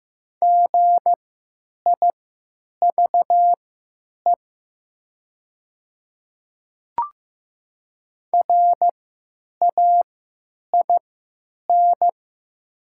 Morse code, heard as a keyed tone, about 15 wpm, spaced as for 9 wpm, 700 Hz. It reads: GIVE RAIN
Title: 3x Character Spacing - Top 300 Words - 15wpm